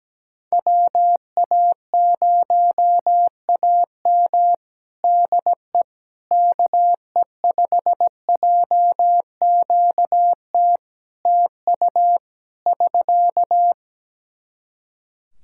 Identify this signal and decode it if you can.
Morse code, heard as a keyed tone, about 17 wpm, 700 Hz. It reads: WA0AM DE KE5JQT TU <SK>
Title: Practice QSO - Exchange 4 - Ending - 17wpm